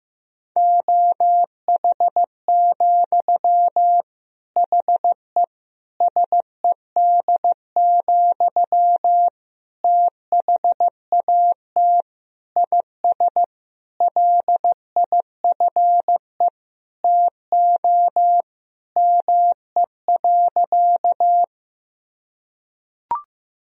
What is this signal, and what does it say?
Morse code, heard as a keyed tone, about 15 wpm, 700 Hz. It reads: OH, HE SED, THAT IS LIFE TO ME.
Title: Sentences from Top 500 Words and Abbreviations 15wpm